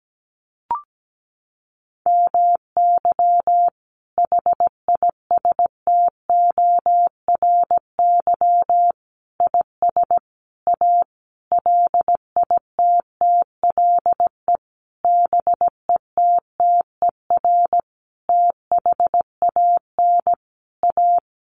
Morse code, heard as a keyed tone, about 17 wpm, 700 Hz. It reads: MY HISTORY IS A LITTLE BETTER THAN A
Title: Sentences from Top 1500 Words 17wpm